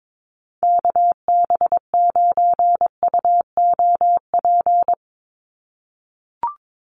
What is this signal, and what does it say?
Morse code, heard as a keyed tone, about 22 wpm, 700 Hz. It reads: KB9UOP
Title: USA Call Signs 22wpm